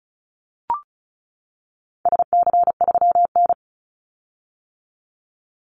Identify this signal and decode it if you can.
Morse code, heard as a keyed tone, about 34 wpm, 700 Hz. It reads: SC3N
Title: Call Signs - 35wpm